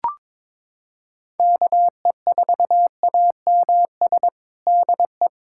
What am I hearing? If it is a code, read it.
Morse code, heard as a keyed tone, about 22 wpm, 700 Hz. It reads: KE4AMS DE
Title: USA Callsign DE Callsign 22wpm